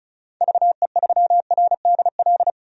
Morse code, heard as a keyed tone, about 36 wpm, 690 Hz. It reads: VE3RDL
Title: DXCC-100 Call Signs - 6 Characters - 35wpm